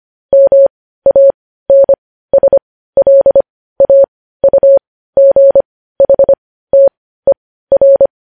Morse code, heard as a keyed tone, about 25 wpm, spaced as for 15 wpm, 550 Hz. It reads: MANSLAUGHTER
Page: news headlines in morse code at 15 wpm